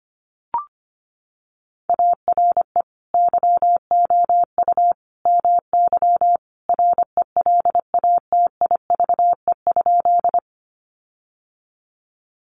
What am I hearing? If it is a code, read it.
Morse code, heard as a keyed tone, about 25 wpm, 700 Hz. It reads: ARE YOU MY RELATIVE?
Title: Sentences from Top 1400 Words 25wpm